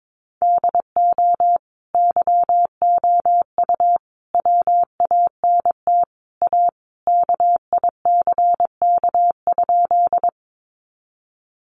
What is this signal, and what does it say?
Morse code, heard as a keyed tone, about 22 wpm, 700 Hz. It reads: DO YOU WANT A KICK?
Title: Sentences from Top 1300 Words 22wpm